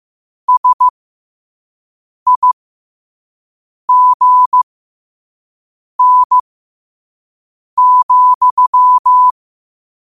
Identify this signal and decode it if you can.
Morse code, heard as a keyed tone, about 15 wpm, spaced as for 5 wpm, 1000 Hz. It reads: SIGN,
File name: qotdfiles / 5wpm